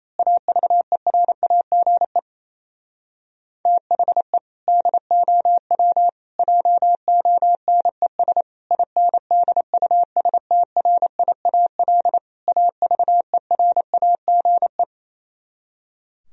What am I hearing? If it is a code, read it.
Morse code, heard as a keyed tone, about 27 wpm, 700 Hz. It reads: AVERAGE THE DOW JONES INDUSTRIAL AVERAGE
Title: Speed-Racing - Sets of 5 Words 28wpm